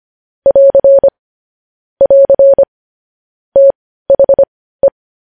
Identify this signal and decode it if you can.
Morse code, heard as a keyed tone, about 25 wpm, spaced as for 15 wpm, 550 Hz. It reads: <AR> <AR> THE
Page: News Headlines in Morse Code at 15 WPM